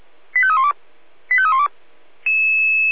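Five-tone selective call (ZVEI), heard as quick runs of five tones, tones of about 70 ms. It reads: 85312 85312
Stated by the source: ZVEI five-tone selective call 85310 received on SDR.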